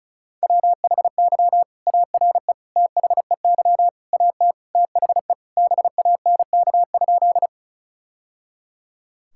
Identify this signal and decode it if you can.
Morse code, heard as a keyed tone, about 36 wpm, 700 Hz. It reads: WHY ARE THEY AT THE BANK?
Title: Sentences from Top 1000 Words - Review Part 6 - 35wpm